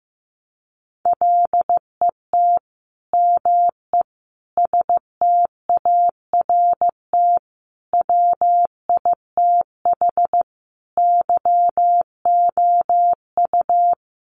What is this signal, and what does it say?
Morse code, heard as a keyed tone, about 15 wpm, 700 Hz. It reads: LET ME START WITH YOU
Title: Sets of 5 Words Encore 15wpm